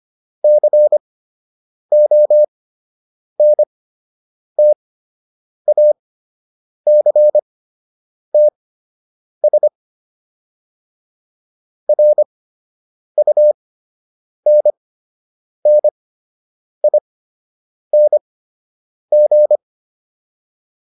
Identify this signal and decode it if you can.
Morse code, heard as a keyed tone, about 25 wpm, spaced as for 8 wpm, 600 Hz. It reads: CONTACTS RUNNING